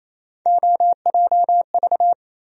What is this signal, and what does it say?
Morse code, heard as a keyed tone, about 28 wpm, 700 Hz. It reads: OJV
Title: USA Call Sign Postfixes - 3 Letters 28wpm